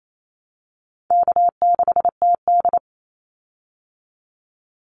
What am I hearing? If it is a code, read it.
Morse code, heard as a keyed tone, about 28 wpm, 700 Hz. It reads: K6TD
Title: Easy Call Signs Repeated 2x - 28wpm